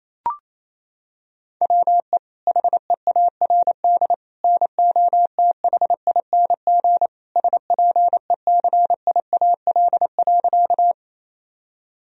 Morse code, heard as a keyed tone, about 28 wpm, 700 Hz. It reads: WE HEARD NOTHING SPECIAL.
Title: Sentences from Top 1000 Words - Review Part 3 - 28wpm